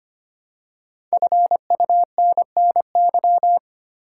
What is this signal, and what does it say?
Morse code, heard as a keyed tone, about 25 wpm, 700 Hz. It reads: FUNNY
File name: 5 Letter Words - First 3 Letters Given - 25wpm